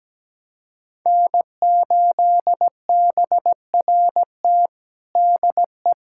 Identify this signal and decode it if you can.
Morse code, heard as a keyed tone, about 17 wpm, 700 Hz. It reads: N8BRT DE
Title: USA Callsign DE Callsign - 5 Characters - 17wpm